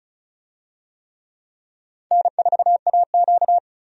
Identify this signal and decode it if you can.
Morse code, heard as a keyed tone, about 34 wpm, 700 Hz. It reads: N4AQ
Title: USA Call Signs - 4 Characters - 35wpm